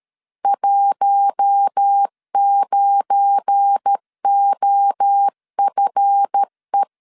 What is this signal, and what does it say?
Morse code, heard as a keyed tone, about 13 wpm, 800 Hz. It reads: 19OFE
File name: LEZIONI / 7 GF9Z / 7 GF9 12wpm 2319